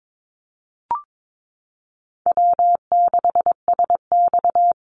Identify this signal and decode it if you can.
Morse code, heard as a keyed tone, about 22 wpm, 700 Hz. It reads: W6SX
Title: Easy Call Signs Repeated 3x - 22wpm